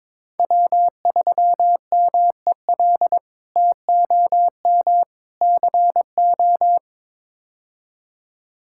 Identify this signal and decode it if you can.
Morse code, heard as a keyed tone, about 22 wpm, 700 Hz. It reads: W3MEL TOM CO
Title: K1USN SST QSO Practice - 22wpm